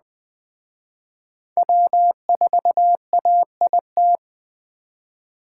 Morse code, heard as a keyed tone, about 20 wpm, 700 Hz. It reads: W4AIT
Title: Speed Racing - USA 1x3 Call Signs 17wpm